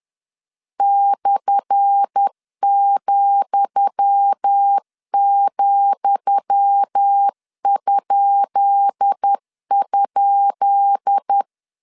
Morse code, heard as a keyed tone, about 10 wpm, 800 Hz. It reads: /,,??
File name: LEZIONI / 10 SIGNS / 10D COMA 10wpm 3013